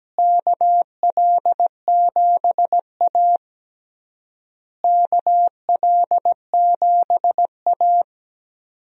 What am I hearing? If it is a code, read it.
Morse code, heard as a keyed tone, about 17 wpm, 700 Hz. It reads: KL7A KL7A
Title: DXCC-100 Call Signs - 4 Characters - Repeated 3x - 17wpm